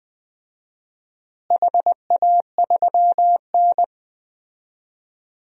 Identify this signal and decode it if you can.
Morse code, heard as a keyed tone, about 19 wpm, 700 Hz. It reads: HA3N
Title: DXCC-100 Call Signs - 4 Characters - Repeated 3x - 20wpm